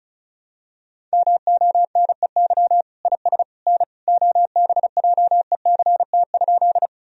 Morse code, heard as a keyed tone, about 34 wpm, 700 Hz. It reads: MONEY IS N OBJECT?
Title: Sentences from Top 500 Words and Abbreviations 35wpm